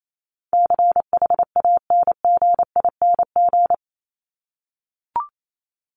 Morse code, heard as a keyed tone, about 28 wpm, 700 Hz. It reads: CHANGING
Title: ING Suffix - Top 100 Words with Base Words 28wpm